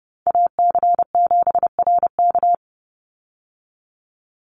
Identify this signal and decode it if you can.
Morse code, heard as a keyed tone, about 30 wpm, 700 Hz. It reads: AC7RK
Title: USA Call Signs 30wpm